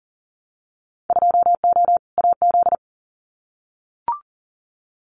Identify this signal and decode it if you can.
Morse code, heard as a keyed tone, about 40 wpm, 700 Hz. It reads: 2O AZ
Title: Winter Field Day Exchanges 40wpm